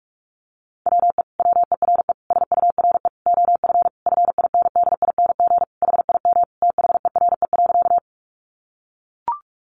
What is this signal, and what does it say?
Morse code, heard as a keyed tone, about 44 wpm, 700 Hz. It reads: WE WERE SURE OF FINDING HIM THERE.
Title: Sentences from Top 1100 Words 45wpm